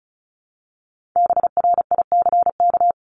Morse code, heard as a keyed tone, about 34 wpm, 700 Hz. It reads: BRICK
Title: Top 1751-2000 Words 35wpm